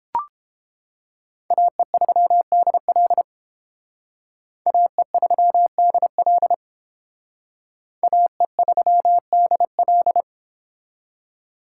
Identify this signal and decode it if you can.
Morse code, heard as a keyed tone, about 30 wpm, 700 Hz. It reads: AE3DL AE3DL AE3DL